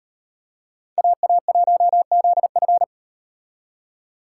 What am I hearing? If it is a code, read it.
Morse code, heard as a keyed tone, about 38 wpm, 700 Hz. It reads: AA1ZF